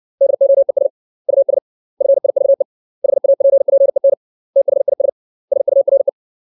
Morse code, heard as a keyed tone, about 60 wpm, 550 Hz. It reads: DOES US FIVE HAPPEN THIS SURE
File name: 632 words / 632 Words 60WPM